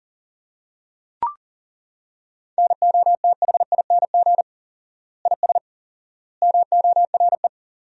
Morse code, heard as a keyed tone, about 40 wpm, 700 Hz. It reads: NOTHING IS MORE
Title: 4x Word-Spacing - Sentences from Top 500 Words - 40wpm